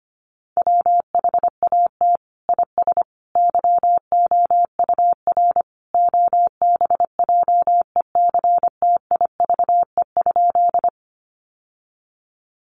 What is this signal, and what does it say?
Morse code, heard as a keyed tone, about 25 wpm, 700 Hz. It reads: WHAT IS YOUR OBJECTIVE?
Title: Sentences from Top 1200 Words 25wpm